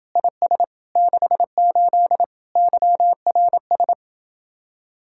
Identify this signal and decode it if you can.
Morse code, heard as a keyed tone, about 31 wpm, 700 Hz. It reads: IS 68 YRS